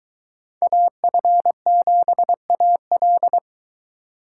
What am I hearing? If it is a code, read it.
Morse code, heard as a keyed tone, about 23 wpm, 700 Hz. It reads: AF7AL